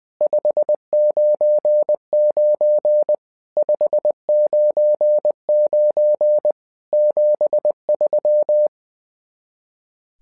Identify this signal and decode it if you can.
Morse code, heard as a keyed tone, about 20 wpm, 600 Hz. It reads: 599 599 73
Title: SOTA QSO Practice 20wpm